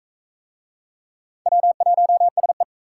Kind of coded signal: Morse code

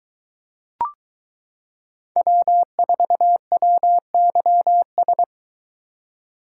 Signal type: Morse code

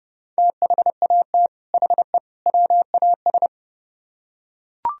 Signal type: Morse code

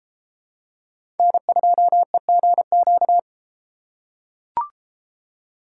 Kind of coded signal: Morse code

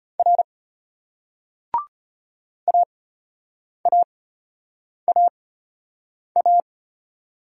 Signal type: Morse code

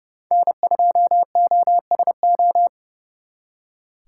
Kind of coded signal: Morse code

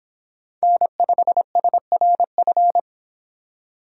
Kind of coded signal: Morse code

